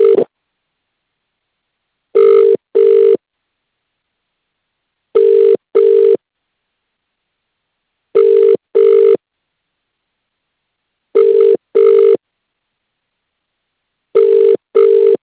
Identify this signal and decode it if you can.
Ringback tone (UK), 400 + 450 Hz.